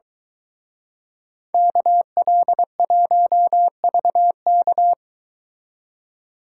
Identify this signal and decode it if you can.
Morse code, heard as a keyed tone, about 23 wpm, 700 Hz. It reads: KL1VK